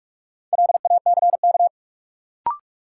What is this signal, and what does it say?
Morse code, heard as a keyed tone, about 43 wpm, 700 Hz. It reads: LACK
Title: Sentences from Top 1100 Words - New Word Drill - 45wpm